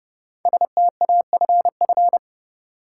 Morse code, heard as a keyed tone, about 29 wpm, 700 Hz. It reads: STAFF
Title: Sentences from Top 1200 Words - New Word Drill - 30wpm